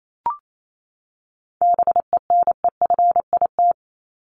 Morse code, heard as a keyed tone, about 28 wpm, 700 Hz. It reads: BENEFIT
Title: Top 751-1000 Words 28wpm